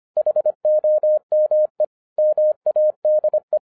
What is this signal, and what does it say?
Morse code, heard as a keyed tone, about 25 wpm, 610 Hz. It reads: HOME MADE
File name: FIRST FISTS CONTACT-25wpm